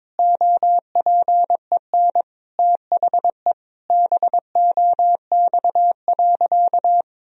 Morse code, heard as a keyed tone, about 22 wpm, 700 Hz. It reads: OPEN THE BOX.
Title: Sentences from Top 1000 Words - Review Part 4 - 22wpm